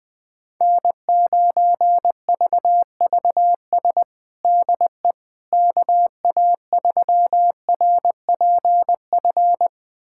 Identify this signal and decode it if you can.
Morse code, heard as a keyed tone, about 20 wpm, 700 Hz. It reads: N9VVS DE KA3RPF
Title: USA Callsign DE Callsign 20wpm